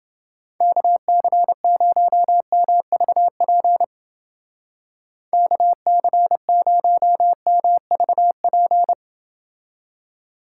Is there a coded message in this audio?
Yes: Morse code from 0.6 to 8.9 s, heard as a keyed tone, about 28 wpm, 700 Hz. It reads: KC0MVP KC0MVP